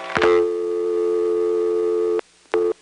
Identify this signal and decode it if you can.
Dial tone (North America), 350 + 440 Hz.